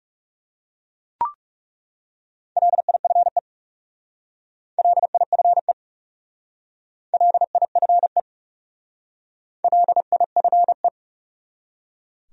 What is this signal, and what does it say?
Morse code, heard as a keyed tone, about 40 wpm, 700 Hz. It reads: LIFE LIFE LIFE LIFE